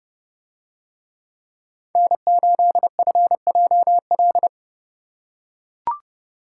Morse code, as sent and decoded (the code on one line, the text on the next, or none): -. ---.. ..-. .--- .-..
N8FJL